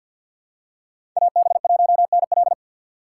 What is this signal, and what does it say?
Morse code, heard as a keyed tone, about 51 wpm, 700 Hz. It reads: AB1NL
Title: USA Call Signs - 3x Character Spacing - 50wpm